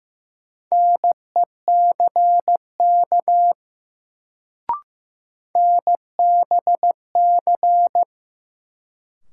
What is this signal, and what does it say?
Morse code, heard as a keyed tone, about 15 wpm, 700 Hz. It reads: NECK NBC
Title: Sets of 5 Words Encore - New Word Drill - 15wpm